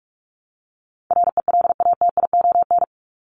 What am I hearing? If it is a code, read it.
Morse code, heard as a keyed tone, about 44 wpm, 700 Hz. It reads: RELATION
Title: ES:S Suffix - Top 100 Words with Base Words 45wpm